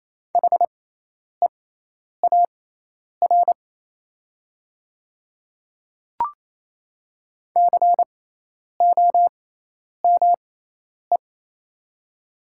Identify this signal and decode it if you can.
Morse code, heard as a keyed tone, about 28 wpm, spaced as for 10 wpm, 700 Hz. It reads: HEAR COME
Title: 6x Character Spacing - Top 300 Words - 28wpm